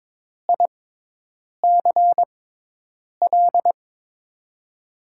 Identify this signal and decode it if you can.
Morse code, heard as a keyed tone, about 21 wpm, spaced as for 8 wpm, 700 Hz. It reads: ICL
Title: USA Call Signs - 6 Characters - 6x Character Spacing - 22wpm